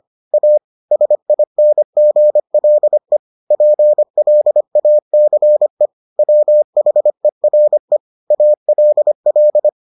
Morse code, heard as a keyed tone, about 25 wpm, 600 Hz. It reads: A SINGLE PLACE WHERE ALL